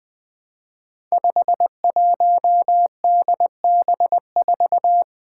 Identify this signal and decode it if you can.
Morse code, heard as a keyed tone, about 20 wpm, 700 Hz. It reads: 51DB4
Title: Five Challenging Letter-Number Combinations 20wpm